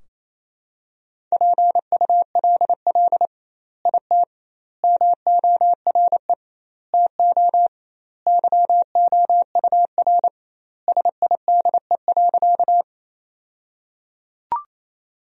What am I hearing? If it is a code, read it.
Morse code, heard as a keyed tone, about 28 wpm, 700 Hz. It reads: PULL IT MORE TO YOUR SIDE.
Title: 2x Word-Spacing - Sentences from Top 400 Words - 28wpm